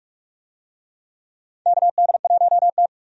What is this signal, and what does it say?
Morse code, heard as a keyed tone, about 42 wpm, 700 Hz. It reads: KD1T
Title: USA Call Signs - 4 Characters - 45wpm